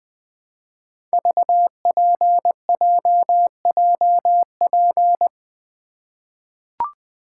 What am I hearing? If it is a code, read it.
Morse code, heard as a keyed tone, about 21 wpm, 700 Hz. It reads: VPJJP
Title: Code Groups - 5 Letters - 20wpm